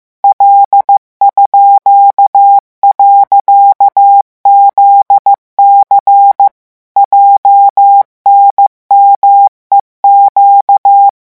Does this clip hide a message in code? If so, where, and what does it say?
Yes: Morse code from 0.2 to 11.1 s, heard as a keyed tone, about 15 wpm, 800 Hz. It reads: L_.ZC JNMEQ